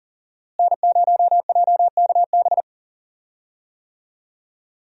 Morse code, heard as a keyed tone, about 40 wpm, 700 Hz. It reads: N0JKB
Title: USA Call Signs - 5 Characters - Repeated 2x - 40wpm